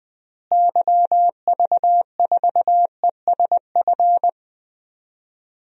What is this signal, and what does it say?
Morse code, heard as a keyed tone, about 20 wpm, 700 Hz. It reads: YV4ESF
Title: DXCC-100 Call Signs - 6 Characters - Repeated 3x - 20wpm